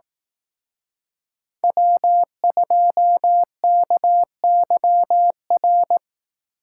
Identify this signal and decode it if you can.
Morse code, heard as a keyed tone, about 18 wpm, 700 Hz. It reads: W2KYR
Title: Speed Racing - USA 1x3 Call Signs 15wpm